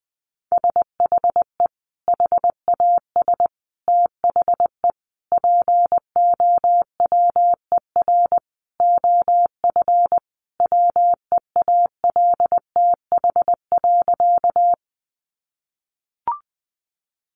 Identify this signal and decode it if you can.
Morse code, heard as a keyed tone, about 20 wpm, 700 Hz. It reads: SHE HAS THE POWER OF WEALTH.
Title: Sentences from Top 1200 Words 20wpm